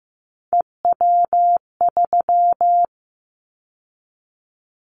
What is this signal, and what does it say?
Morse code, heard as a keyed tone, about 15 wpm, 700 Hz. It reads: EW3
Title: Callsign Prefixes for Top 50 DXCC - 15wpm